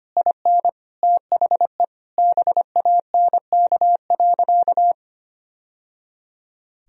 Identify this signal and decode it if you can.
Morse code, heard as a keyed tone, about 25 wpm, 700 Hz. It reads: IN THE BANK.
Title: Sentences from Top 1000 Words - Review Part 4 - 25wpm